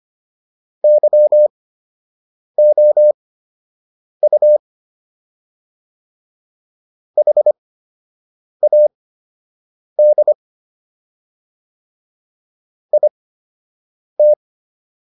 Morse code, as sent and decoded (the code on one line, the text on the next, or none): -.-- --- ..- / .... .- -.. / .. -
YOU HAD IT